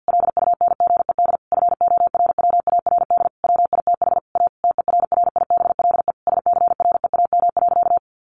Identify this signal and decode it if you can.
Morse code, heard as a keyed tone, about 49 wpm, 700 Hz. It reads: LUNGED FORWARD WITH A TERRIBLE SCREAM.
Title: A Sound Of Thunder 50wpm